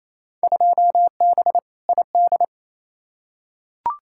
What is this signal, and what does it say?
Morse code, heard as a keyed tone, about 28 wpm, 700 Hz. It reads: 2B ID
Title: Field Day Exchanges Encore - 28wpm